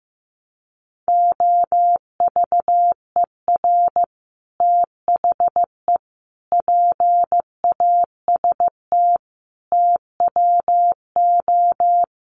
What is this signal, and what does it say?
Morse code, heard as a keyed tone, about 15 wpm, 700 Hz. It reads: OVER THE PAST TWO
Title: Sets of 5 Words Encore2 15wpm